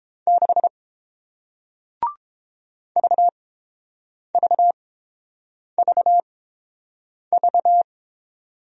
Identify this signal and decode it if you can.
Morse code, heard as a keyed tone, about 29 wpm, spaced as for 7 wpm, 700 Hz. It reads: 6 VVVV